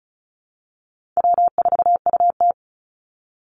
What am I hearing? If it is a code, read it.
Morse code, heard as a keyed tone, about 34 wpm, 700 Hz. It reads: W4UT
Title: Easy Call Signs Repeated 2x - 35wpm